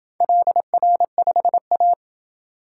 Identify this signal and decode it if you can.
Morse code, heard as a keyed tone, about 26 wpm, 700 Hz. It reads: LR5A